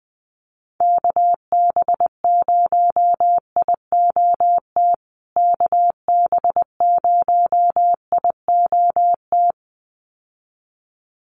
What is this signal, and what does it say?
Morse code, heard as a keyed tone, about 20 wpm, 700 Hz. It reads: KB0IOT KB0IOT
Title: USA Call Signs Repeated 2x 20wpm